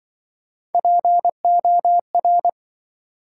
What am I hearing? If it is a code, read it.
Morse code, heard as a keyed tone, about 27 wpm, 700 Hz. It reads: POR